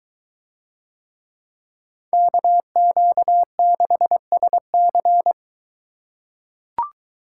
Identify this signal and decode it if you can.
Morse code, heard as a keyed tone, about 22 wpm, 700 Hz. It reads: KQ6SC